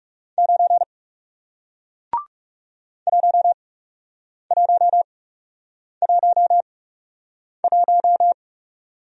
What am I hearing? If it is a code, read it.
Morse code, heard as a keyed tone, about 40 wpm, spaced as for 8 wpm, 700 Hz. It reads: 9 1111